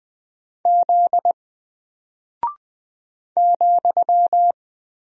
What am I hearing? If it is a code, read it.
Morse code, heard as a keyed tone, about 20 wpm, 700 Hz. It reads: Z ,
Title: Single Character (Letter:Number:Punctuation:Prosign) Alt 20wpm